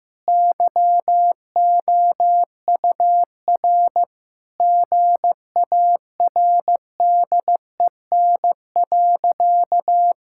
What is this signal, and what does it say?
Morse code, heard as a keyed tone, about 15 wpm, 700 Hz. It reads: YOUR GARDEN.
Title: Sentences from Top 1000 Words - Review Part 5 - 15wpm